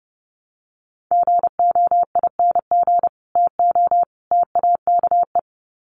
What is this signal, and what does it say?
Morse code, heard as a keyed tone, about 30 wpm, 700 Hz. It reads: GOING TO TAKE
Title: Sets of 3 Words 30wpm